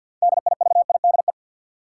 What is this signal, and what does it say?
Morse code, heard as a keyed tone, about 51 wpm, 700 Hz. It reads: DIVIDE
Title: Sentences from Top 1600 Words - New Word Drill Alt - 50wpm